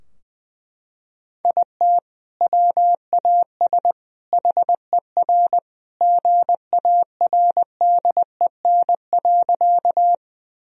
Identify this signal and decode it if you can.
Morse code, heard as a keyed tone, about 20 wpm, 700 Hz. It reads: IT WAS HER GARDEN.
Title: Sentences from Top 1000 Words - Review Part 4 - 20wpm